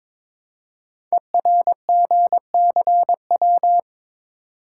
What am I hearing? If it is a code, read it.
Morse code, heard as a keyed tone, about 22 wpm, 700 Hz. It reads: ERGCW
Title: Code Groups - 5 Letter:Numbers - 22wpm